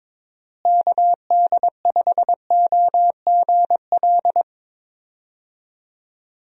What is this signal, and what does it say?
Morse code, heard as a keyed tone, about 22 wpm, 700 Hz. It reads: KD5OGL
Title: USA Call Signs - 6 Characters - Repeated 3x - 22wpm